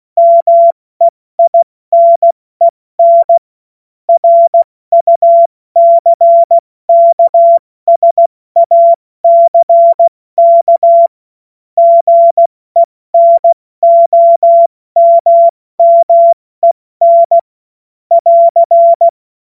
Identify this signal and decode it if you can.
Morse code, heard as a keyed tone, about 16 wpm, 680 Hz. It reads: MEINEN RUCKSACK GENOMMEN <AR>
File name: Morsekurs / Witze / Witz 16. Februar 2021, 70 F80